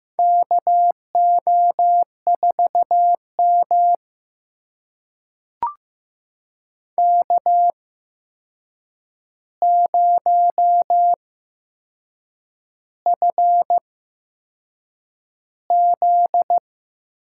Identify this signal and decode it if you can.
Morse code, heard as a keyed tone, about 15 wpm, spaced as for 6 wpm, 700 Hz. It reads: KO4M K0FZ
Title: USA Call Signs - 4 Characters - 8x Character Spacing - 15wpm